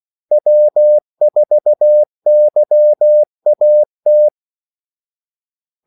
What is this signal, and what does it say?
Morse code, heard as a keyed tone, about 16 wpm, 600 Hz. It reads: W4YAT
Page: Kaldesignaler 16 wpm